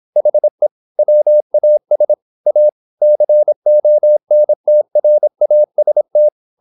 Morse code, heard as a keyed tone, about 26 wpm, 600 Hz. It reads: HE WAS A CONTRAST
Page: Never 26wpm | CW med Gnister